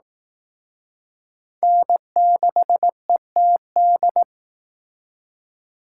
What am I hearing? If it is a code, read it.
Morse code, heard as a keyed tone, about 18 wpm, 700 Hz. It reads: N6ETD
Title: Speed Racing - USA 1x3 Call Signs 15wpm